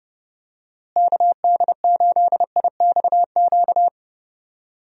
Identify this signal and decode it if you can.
Morse code, heard as a keyed tone, about 30 wpm, 700 Hz. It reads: KD8IXQ
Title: USA Call Signs - 6 Characters - Repeated 2x - 30wpm